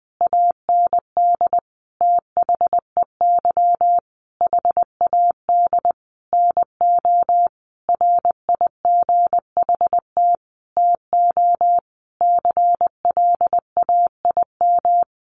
Morse code, heard as a keyed tone, about 20 wpm, 700 Hz. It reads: AND THEY HAD NO RIGHT TO CLAIM
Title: Sentences from Top 1500 Words 20wpm